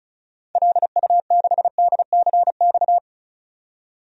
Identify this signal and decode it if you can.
Morse code, heard as a keyed tone, about 34 wpm, 700 Hz. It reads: LU6DCX
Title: DXCC-100 Call Signs - 6 Characters - 35wpm